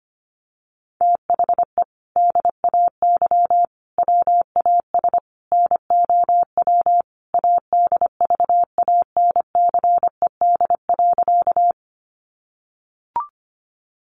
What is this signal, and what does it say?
Morse code, heard as a keyed tone, about 25 wpm, 700 Hz. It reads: THE DAY WAS NOW ADVANCED.
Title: Sentences from Top 1200 Words 25wpm